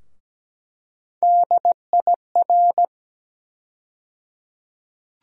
Morse code, heard as a keyed tone, about 17 wpm, 700 Hz. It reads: DIR
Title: USA Call Sign Postfixes - 3 Letters 17wpm